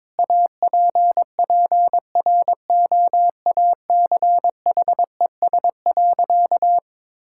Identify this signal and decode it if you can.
Morse code, heard as a keyed tone, about 22 wpm, 700 Hz. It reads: APPROACHES.
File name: Difficult Sentences with Hints - 22wpm